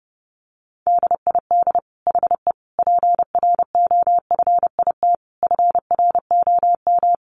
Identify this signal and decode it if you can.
Morse code, heard as a keyed tone, about 30 wpm, 700 Hz. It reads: DID HE PROFIT FROM
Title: Sentences from Top 800 Words 30wpm